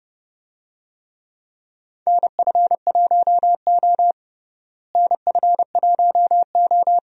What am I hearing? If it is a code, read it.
Morse code, heard as a keyed tone, about 31 wpm, 700 Hz. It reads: NF1O NF1O
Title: USA Call Signs - 4 Characters - Repeated 2x - 30wpm